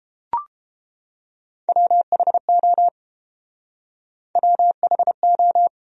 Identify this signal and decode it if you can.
Morse code, heard as a keyed tone, about 29 wpm, 700 Hz. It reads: WHO WHO